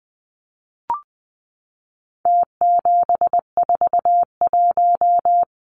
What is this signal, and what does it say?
Morse code, heard as a keyed tone, about 20 wpm, 700 Hz. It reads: T741
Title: Numbers - Contest Style - 4-Digit 20wpm